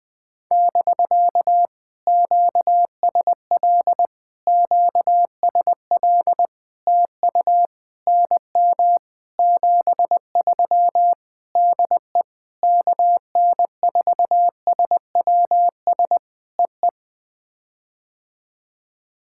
Morse code, heard as a keyed tone, about 20 wpm, 700 Hz. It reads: <BK> QSL QSL TU NM 73 DE KN4SWS EE
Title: POTA QSO Practice - 20wpm